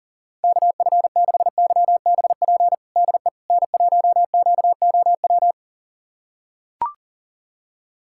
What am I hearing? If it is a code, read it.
Morse code, heard as a keyed tone, about 40 wpm, 700 Hz. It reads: KF6YBP DE N1QOW
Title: USA Callsign DE Callsign 40wpm